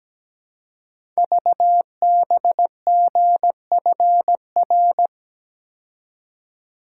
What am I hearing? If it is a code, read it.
Morse code, heard as a keyed tone, about 17 wpm, 700 Hz. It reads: VBGFR
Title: Code Groups - 5 Letters - 17wpm